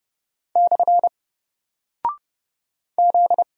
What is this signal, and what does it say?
Morse code, heard as a keyed tone, about 32 wpm, 700 Hz. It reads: / Z
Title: Single Character (Letter:Number:Punctuation:Prosign) 30wpm